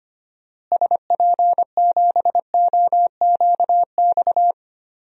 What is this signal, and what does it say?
Morse code, heard as a keyed tone, about 25 wpm, 700 Hz. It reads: SP7OQX
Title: DXCC-100 Call Signs - 6 Characters - Repeated 2x - 25wpm